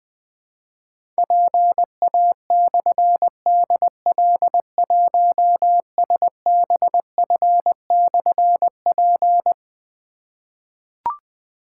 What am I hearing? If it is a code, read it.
Morse code, heard as a keyed tone, about 20 wpm, 700 Hz. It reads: PA/DL1SBF/P
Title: Difficult Call Signs 20wpm